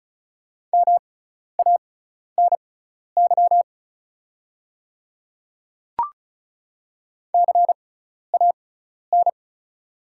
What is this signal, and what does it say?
Morse code, heard as a keyed tone, about 34 wpm, spaced as for 12 wpm, 690 Hz. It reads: MANY CAN
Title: 6x Character Spacing - Top 500 Words - 35wpm